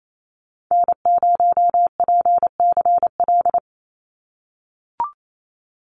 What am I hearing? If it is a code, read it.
Morse code, heard as a keyed tone, about 28 wpm, 700 Hz. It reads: N0PCL
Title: NA SOTA Activator Callsigns 28wpm